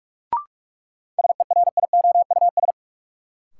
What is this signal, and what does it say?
Morse code, heard as a keyed tone, about 45 wpm, 700 Hz. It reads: SERIOUS
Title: Sentences from Top 1100 Words - New Word Drill - 45wpm